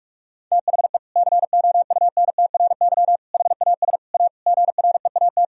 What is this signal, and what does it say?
Morse code, heard as a keyed tone, about 41 wpm, 700 Hz. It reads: THE COUNTRY HAS A GREAT
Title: Sentences from Top 1000 Words - Review Part 4 - 45wpm